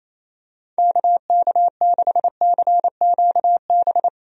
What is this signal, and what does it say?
Morse code, heard as a keyed tone, about 28 wpm, 700 Hz. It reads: KK6CQB
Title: USA Call Signs - 6 Characters - Repeated 2x - 28wpm